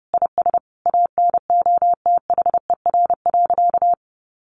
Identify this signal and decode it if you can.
Morse code, heard as a keyed tone, about 30 wpm, 700 Hz. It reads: IS ANOTHER.
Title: Sentences from Top 1000 Words 30wpm